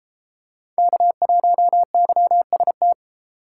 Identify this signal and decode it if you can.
Morse code, heard as a keyed tone, about 32 wpm, 700 Hz. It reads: K1YST